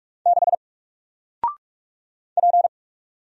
Morse code, heard as a keyed tone, about 44 wpm, 700 Hz. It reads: 6 P
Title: Single Letter-Number 45wpm